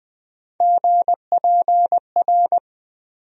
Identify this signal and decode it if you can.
Morse code, heard as a keyed tone, about 20 wpm, 700 Hz. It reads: GPR
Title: USA Call Sign Postfixes - 3 Letters 20wpm